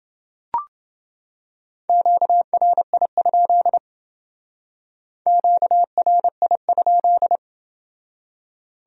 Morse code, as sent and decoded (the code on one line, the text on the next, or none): --.- .-. .. ..--.. / --.- .-. .. ..--..
QRI? QRI?